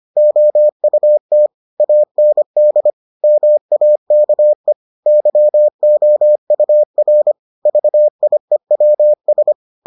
Morse code, heard as a keyed tone, about 25 wpm, 600 Hz. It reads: OUT AND MAKE YOUR VIEWS